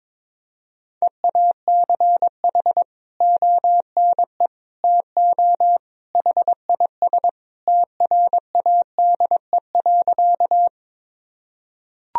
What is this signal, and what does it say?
Morse code, heard as a keyed tone, about 22 wpm, 700 Hz. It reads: EACH ONE TO HIS TRADE.
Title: Sentences from Top 1000 Words - Review Part 4 - 22wpm